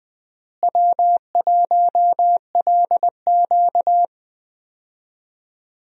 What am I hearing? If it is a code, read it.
Morse code, heard as a keyed tone, about 21 wpm, 700 Hz. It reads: W1LQ
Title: USA Call Signs - 4 Characters - Repeated 3x - 20wpm